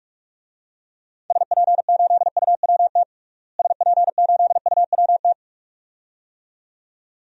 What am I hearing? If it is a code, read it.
Morse code, heard as a keyed tone, about 46 wpm, 700 Hz. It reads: SP8UWT SP8UWT
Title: DXCC-100 Call Signs - 6 Characters - Repeated 2x - 45wpm